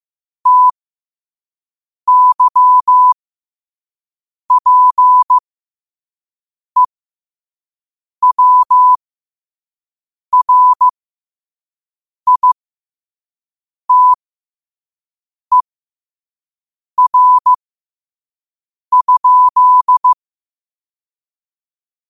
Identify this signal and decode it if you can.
Morse code, heard as a keyed tone, about 15 wpm, spaced as for 5 wpm, 1000 Hz. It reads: TYPEWRITER?
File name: qotdfiles / 5wpm